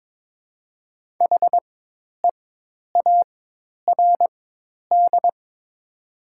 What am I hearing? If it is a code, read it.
Morse code, heard as a keyed tone, about 22 wpm, spaced as for 10 wpm, 700 Hz. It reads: HEARD
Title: 4x Character Spacing - Top 500 Words - 22wpm